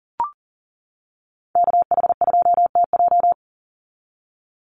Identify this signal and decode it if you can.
Morse code, heard as a keyed tone, about 40 wpm, 700 Hz. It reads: KH2TJ
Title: NA SOTA Chaser Callsigns 40wpm